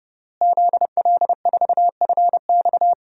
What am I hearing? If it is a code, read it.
Morse code, heard as a keyed tone, about 31 wpm, 700 Hz. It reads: ZL4FX
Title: DXCC-100 Call Signs - 5 Characters - Repeated 3x - 30wpm